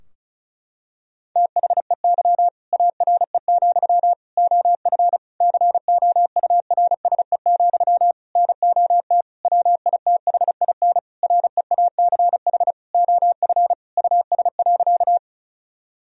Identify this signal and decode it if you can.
Morse code, heard as a keyed tone, about 36 wpm, 700 Hz. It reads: THEY ARE, OF COURSE, NOT WITHIN REACH OF US.
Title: Sentences from Top 1000 Words - Review Part 4 - 35wpm